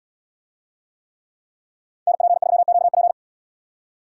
Morse code, heard as a keyed tone, about 76 wpm, 700 Hz. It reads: NP3QP